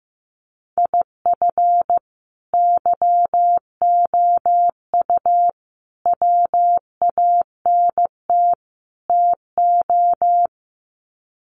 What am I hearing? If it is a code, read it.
Morse code, heard as a keyed tone, about 15 wpm, 700 Hz. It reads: IF YOU WANT TO
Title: Sets of 4 Words 15wpm